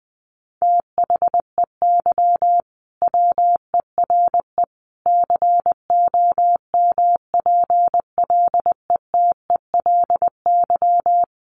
Morse code, heard as a keyed tone, about 20 wpm, 700 Hz. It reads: THEY WERE COMPLETELY
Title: Sentences from Top 1100 Words 20wpm